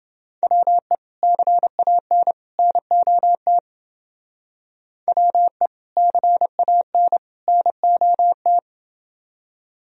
Morse code, heard as a keyed tone, about 29 wpm, 700 Hz. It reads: WE CAN NOT WE CAN NOT